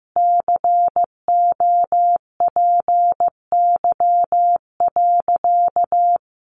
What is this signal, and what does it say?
Morse code, heard as a keyed tone, about 15 wpm, 700 Hz. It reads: COPY.
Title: Sentences from Top 1400 Words 15wpm